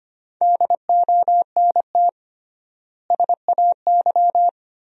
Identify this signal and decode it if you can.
Morse code, heard as a keyed tone, about 25 wpm, 700 Hz. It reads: DONT SAY
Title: 3x Word-Spacing - Sentences from Top 300 Words - 25wpm